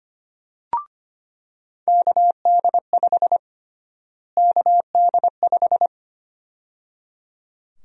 Morse code, heard as a keyed tone, about 25 wpm, 700 Hz. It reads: KD5 KD5
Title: USA Call Sign Prefixes - Repeated 2x - 25wpm